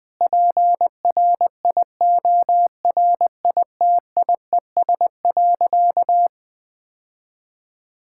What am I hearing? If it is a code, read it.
Morse code, heard as a keyed tone, about 20 wpm, 700 Hz. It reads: PRIORITIES.
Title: Sentences with 7-Word Phrases - 20wpm